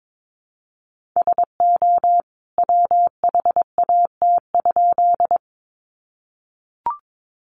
Morse code, heard as a keyed tone, about 22 wpm, 700 Hz. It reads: SO WHAT?
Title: A Sound Of Thunder 22wpm